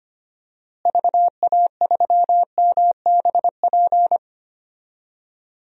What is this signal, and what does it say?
Morse code, heard as a keyed tone, about 25 wpm, 700 Hz. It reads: VA3MBP
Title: DXCC-100 Call Signs - 6 Characters - Repeated 2x - 25wpm